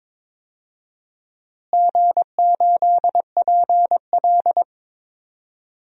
Morse code, heard as a keyed tone, about 22 wpm, 700 Hz. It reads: G8PL
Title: DXCC-100 Call Signs - 4 Characters - Repeated 3x - 22wpm